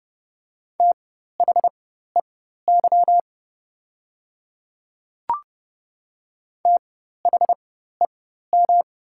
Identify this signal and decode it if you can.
Morse code, heard as a keyed tone, about 30 wpm, spaced as for 14 wpm, 690 Hz. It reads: THEY THEM
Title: 4x Character Spacing - Top 100 Words - 30wpm